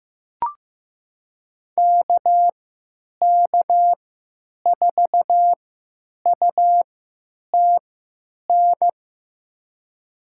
Morse code, heard as a keyed tone, about 15 wpm, spaced as for 9 wpm, 700 Hz. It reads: KK4UTN
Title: USA Call Signs - 6 Characters - 3x Character Spacing - 15wpm